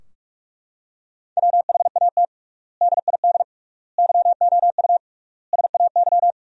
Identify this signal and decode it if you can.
Morse code, heard as a keyed tone, about 41 wpm, 700 Hz. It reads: WHAT DID YOU SAY
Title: 3x Word-Spacing - Sentences from Top 200 Words - 45wpm